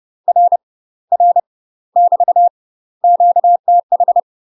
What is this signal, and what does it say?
Morse code, heard as a keyed tone, about 30 wpm, 700 Hz. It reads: R R <BT> QTH
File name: UK MASTER QSOs TRAINING / 30WPM / UK MASTER QSOs 0021